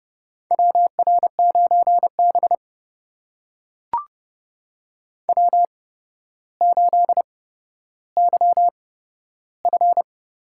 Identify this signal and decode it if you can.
Morse code, heard as a keyed tone, about 29 wpm, spaced as for 13 wpm, 700 Hz. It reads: WR9B W8YF
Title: USA Call Signs - 4 Characters - 8x Character Spacing - 30wpm